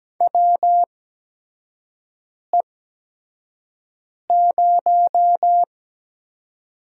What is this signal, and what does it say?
Morse code, heard as a keyed tone, about 17 wpm, spaced as for 5 wpm, 700 Hz. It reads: WE0
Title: USA Call Signs - 4 Characters - 8x Character Spacing - 17wpm